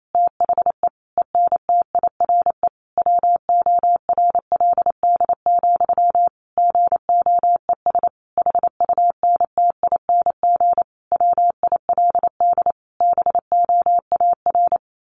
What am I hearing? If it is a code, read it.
Morse code, heard as a keyed tone, about 28 wpm, 700 Hz. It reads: THE ENTIRE WORLD, GOES HUNTING WILD BOAR
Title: A Sound Of Thunder 28wpm